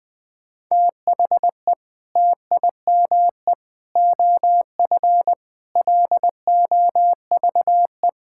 Morse code, heard as a keyed tone, about 20 wpm, 700 Hz. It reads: THE TIME OF LOVE
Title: Sentences from Top 1600 Words 20wpm